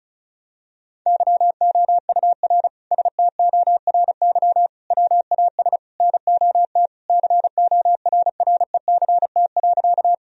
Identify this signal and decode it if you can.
Morse code, heard as a keyed tone, about 36 wpm, 700 Hz. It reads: YOUR STORY WAS NOT CORRECT.
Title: Sentences from Top 1000 Words - Review Part 2 - 35wpm